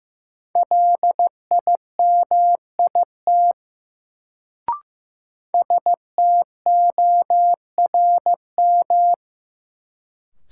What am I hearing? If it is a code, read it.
Morse code, heard as a keyed tone, about 15 wpm, 700 Hz. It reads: LIMIT STORM
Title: Sentences from Top 1500 Words - New Word Drill - 15wpm